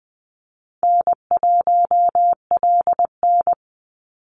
Morse code, heard as a keyed tone, about 20 wpm, 700 Hz. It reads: N1LN
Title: Call Signs Repeated 2x - 20wpm